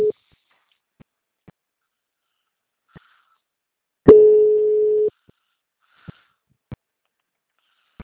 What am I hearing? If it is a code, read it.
Ringback tone (Europe, 425 Hz).